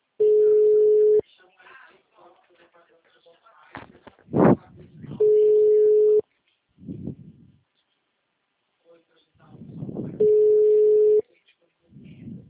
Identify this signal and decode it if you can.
Ringback tone (Europe, 425 Hz), with speech.